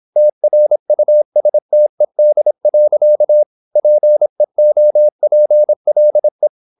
Morse code, heard as a keyed tone, about 26 wpm, 600 Hz. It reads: TRUSTED. PEOPLE
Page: Never 26wpm | CW med Gnister